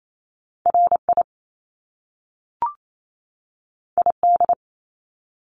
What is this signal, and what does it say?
Morse code, heard as a keyed tone, about 28 wpm, 700 Hz. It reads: RI ID
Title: Top Two Letter Combinations in Words 28wpm